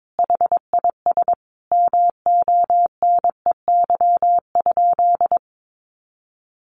Morse code, heard as a keyed tone, about 22 wpm, 700 Hz. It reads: HIS MONEY?
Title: Sentences from Top 400 Words 22wpm